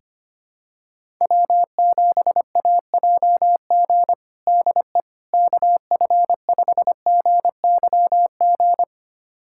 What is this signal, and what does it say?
Morse code, heard as a keyed tone, about 25 wpm, 700 Hz. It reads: W7AJG DE KF5GYG
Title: USA Callsign DE Callsign 25wpm